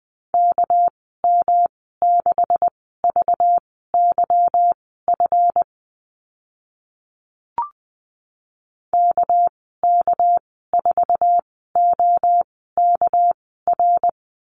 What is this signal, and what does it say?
Morse code, heard as a keyed tone, about 20 wpm, spaced as for 14 wpm, 700 Hz. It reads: KM6VYF KK4OKR
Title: 207 Lesson 33g - Letter j - Call Signs 20wpm